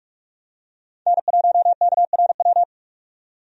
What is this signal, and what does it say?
Morse code, heard as a keyed tone, about 47 wpm, 700 Hz. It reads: N1KRW
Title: USA Call Signs - 5 Characters - 45wpm